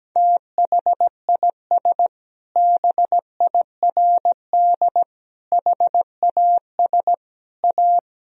Morse code, heard as a keyed tone, about 17 wpm, 700 Hz. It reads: THIS BIRD HAS A
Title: Sentences from Top 1000 Words - Review Part 2 - 17wpm